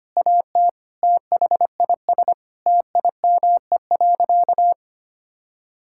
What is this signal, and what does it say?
Morse code, heard as a keyed tone, about 25 wpm, 700 Hz. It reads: AT THIS TIME.
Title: Sentences from Top 1000 Words - Review Part 5 - 25wpm